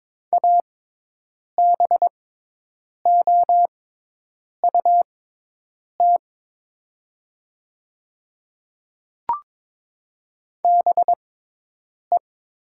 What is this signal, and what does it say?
Morse code, heard as a keyed tone, about 22 wpm, spaced as for 8 wpm, 700 Hz. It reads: ABOUT BE
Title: 6x Character Spacing - Top 100 Words - 22wpm